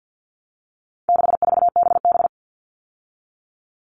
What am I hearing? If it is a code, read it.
Morse code, heard as a keyed tone, about 50 wpm, 700 Hz. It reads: 64BB
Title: All Cut Numbers - 4-Digit 50wpm